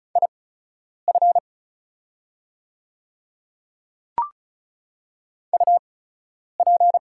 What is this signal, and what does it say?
Morse code, heard as a keyed tone, about 35 wpm, spaced as for 10 wpm, 690 Hz. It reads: IF UP
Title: 8x Character Spacing - Top 100 Words - 35wpm